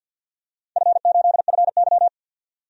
Morse code, heard as a keyed tone, about 47 wpm, 700 Hz. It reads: F8FY
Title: DXCC-100 Call Signs - 4 Characters - 50wpm